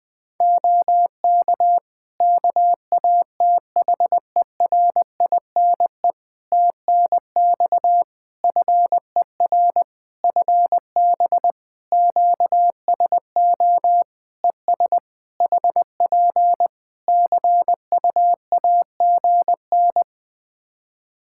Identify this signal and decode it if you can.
Morse code, heard as a keyed tone, about 20 wpm, 700 Hz. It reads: OK KATHERINE TNX FER FB QSO ES HP CUAGN
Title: K1USN SST QSO Practice 20wpm